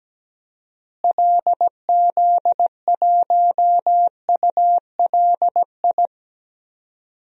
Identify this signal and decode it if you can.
Morse code, heard as a keyed tone, about 17 wpm, 700 Hz. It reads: LZ1ULI
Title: DXCC-100 Call Signs - 6 Characters - Repeated 2x - 17wpm